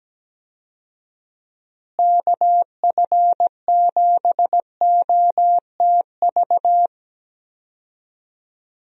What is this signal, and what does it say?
Morse code, heard as a keyed tone, about 17 wpm, 700 Hz. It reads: KF7OTV
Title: USA Call Signs - 6 Characters - Repeated 2x - 17wpm